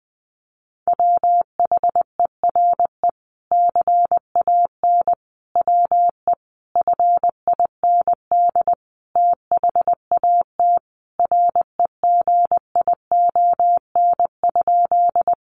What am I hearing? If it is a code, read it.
Morse code, heard as a keyed tone, about 20 wpm, 700 Hz. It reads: WHERE CAN WE FIND THAT REGION?
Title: Sentences from Top 1500 Words 20wpm